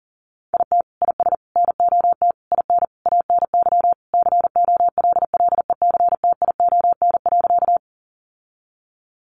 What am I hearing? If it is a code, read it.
Morse code, heard as a keyed tone, about 40 wpm, 700 Hz. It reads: IT IS NOT IN ANY COLLECTION.
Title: Sentences from Top 1200 Words 40wpm